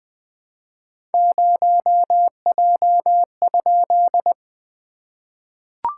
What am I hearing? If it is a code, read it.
Morse code, heard as a keyed tone, about 21 wpm, 690 Hz. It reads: 0J?
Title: SOTA QSO Practice 20wpm